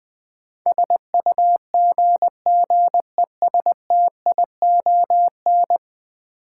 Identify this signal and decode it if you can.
Morse code, heard as a keyed tone, about 20 wpm, 700 Hz. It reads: SUGGESTION
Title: Sentences from Top 1600 Words - New Word Drill Alt - 20wpm